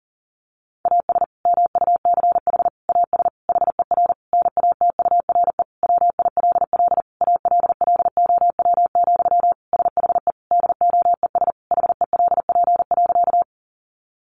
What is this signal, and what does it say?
Morse code, heard as a keyed tone, about 40 wpm, 700 Hz. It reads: AS MUCH AS HER NATURE WILL ALLOW, SHE DOES HELP.
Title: Sentences from Top 700 Words 40wpm